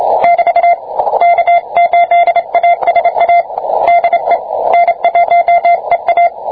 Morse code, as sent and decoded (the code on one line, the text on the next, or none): -...- / -.- ---.. .- ... .- / -.. / -. .---- . .-
<BT> K8ASA DE N1EA